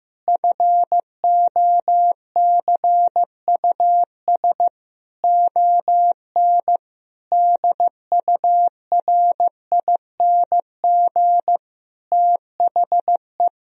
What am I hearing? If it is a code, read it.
Morse code, heard as a keyed tone, about 15 wpm, 700 Hz. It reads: FOCUS ON DURING THE
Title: Sentences from Top 1000 Words - Review Part 4 - 15wpm